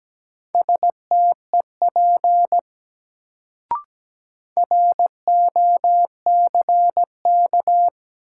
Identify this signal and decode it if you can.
Morse code, heard as a keyed tone, about 17 wpm, 700 Hz. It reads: STEP ROCK
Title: Sentences from Top 400 Words - New Word Drill - 17wpm